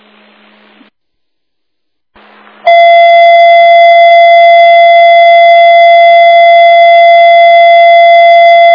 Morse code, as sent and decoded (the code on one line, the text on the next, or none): none